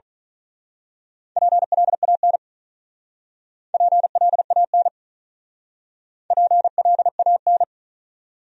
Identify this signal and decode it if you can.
Morse code, heard as a keyed tone, about 39 wpm, 700 Hz. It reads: PLAN PLAN PLAN
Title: Speed-Racing Top 500 Words 35wpm